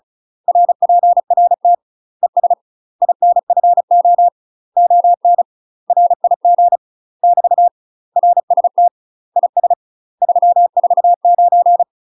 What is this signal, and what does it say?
Morse code, heard as a keyed tone, about 34 wpm, 700 Hz. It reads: RPRT ES INFO ON RIG <BT> RST IS 349
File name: UK MASTER QSOs TRAINING / 35WPM / UK MASTER QSOs 0001